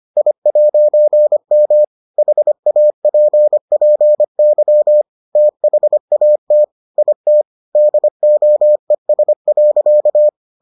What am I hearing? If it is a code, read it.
Morse code, heard as a keyed tone, about 25 wpm, 600 Hz. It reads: I'M HAPPY THAT IT DOES.